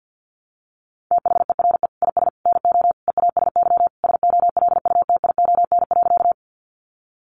Morse code, heard as a keyed tone, about 50 wpm, 700 Hz. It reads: THERE IS NO EASY SOLUTION.
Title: Sentences from Top 1300 Words 50wpm